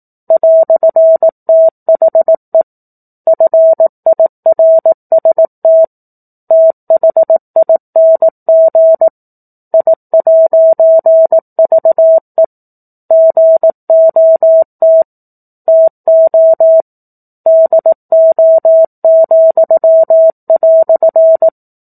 Morse code, heard as a keyed tone, about 18 wpm, 650 Hz. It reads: "THE FIRST THING I'VE GOT TO DO,"
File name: 18 WPM